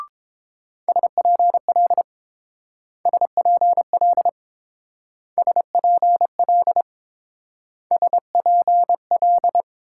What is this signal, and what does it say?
Morse code, heard as a keyed tone, about 28 wpm, 700 Hz. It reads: SPL SPL SPL SPL